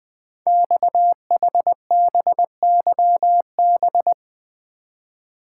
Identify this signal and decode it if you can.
Morse code, heard as a keyed tone, about 20 wpm, 700 Hz. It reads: XHBYB
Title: Code Groups - 5 Letters - 20wpm